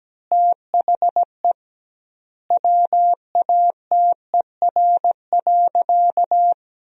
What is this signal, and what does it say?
Morse code, heard as a keyed tone, about 17 wpm, 700 Hz. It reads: THE WATER.
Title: 2x Word-Spacing - Sentences from Top 500 Words - 17wpm